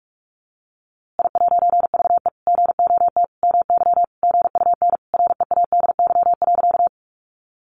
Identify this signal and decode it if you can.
Morse code, heard as a keyed tone, about 44 wpm, 700 Hz. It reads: I'VE GOT MY GUN READY.
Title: A Sound Of Thunder 45wpm